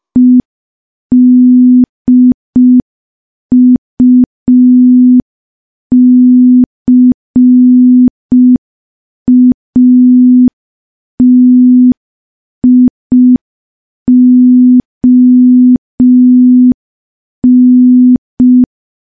Morse code, heard as a keyed tone, about 5 wpm, 250 Hz. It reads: EDUCATION
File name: morse / morse-training-05WPM